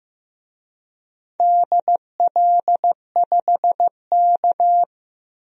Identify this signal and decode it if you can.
Morse code, heard as a keyed tone, about 15 wpm, 700 Hz. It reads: DL5K
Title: DXCC-100 Call Signs - 4 Characters - Repeated 2x - 15wpm